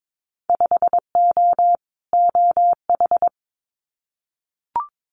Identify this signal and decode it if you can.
Morse code, heard as a keyed tone, about 22 wpm, 700 Hz. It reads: 5O OH
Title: Winter Field Day Exchanges Encore 22wpm